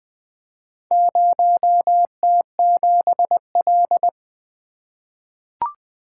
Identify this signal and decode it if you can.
Morse code, heard as a keyed tone, about 20 wpm, 700 Hz. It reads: 0T7L
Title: Four Letter-Number Combinations - 20wpm